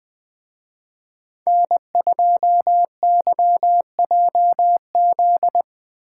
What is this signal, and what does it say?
Morse code, heard as a keyed tone, about 21 wpm, 700 Hz. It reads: N2YJZ